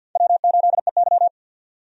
Morse code, heard as a keyed tone, about 50 wpm, 700 Hz. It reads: W8EY
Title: USA Call Signs - 4 Characters - 50wpm